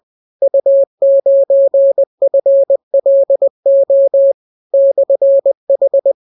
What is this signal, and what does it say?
Morse code, heard as a keyed tone, about 20 wpm, 550 Hz. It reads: U9FLO /H